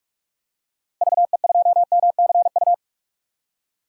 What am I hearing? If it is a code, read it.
Morse code, heard as a keyed tone, about 46 wpm, 700 Hz. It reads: VE2MCU
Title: DXCC-100 Call Signs - 6 Characters - Repeated 3x - 45wpm